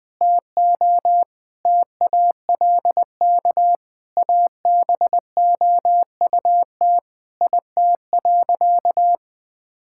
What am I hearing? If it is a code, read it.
Morse code, heard as a keyed tone, about 21 wpm, 700 Hz. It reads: TO TALK ABOUT IT.
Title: Sentences with 5-Word Phrases - 20wpm